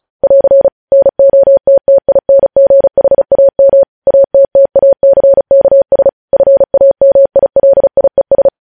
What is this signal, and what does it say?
Morse code, heard as a keyed tone, about 36 wpm, 550 Hz. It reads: <AR> NOTTINGHAM ATTACKS FAMILIES
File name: podcast / News headlines-1x-35x35